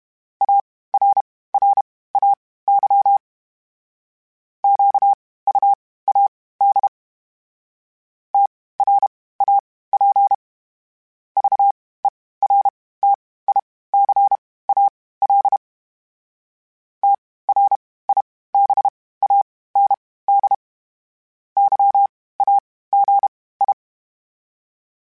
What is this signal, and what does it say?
Morse code, heard as a keyed tone, about 32 wpm, spaced as for 18 wpm, 800 Hz. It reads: ARRAY QUAD TRAP VERTICAL TRIBAND YAGI
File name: (spaced) / FromSMs spaced mp3s / Antennas sp@30